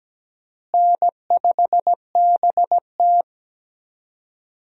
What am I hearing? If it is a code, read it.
Morse code, heard as a keyed tone, about 17 wpm, 700 Hz. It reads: N5BT
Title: USA Call Signs - 4 Characters - Repeated 2x - 17wpm